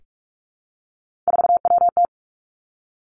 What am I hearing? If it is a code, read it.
Morse code, heard as a keyed tone, about 46 wpm, 700 Hz. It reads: 4WT
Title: Three Challenging Letter-Number Combinations 45wpm